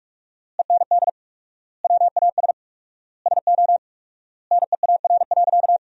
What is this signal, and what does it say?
Morse code, heard as a keyed tone, about 41 wpm, 700 Hz. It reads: END WAS SO NEAR.
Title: 4x Word-Spacing - Sentences from Top 300 Words - 45wpm